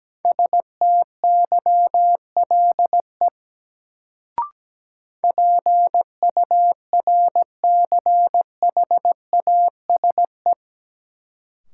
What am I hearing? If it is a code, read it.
Morse code, heard as a keyed tone, about 17 wpm, 700 Hz. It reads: STYLE PURCHASE
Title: Sentences from Top 800 Words - New Word Drill - 17wpm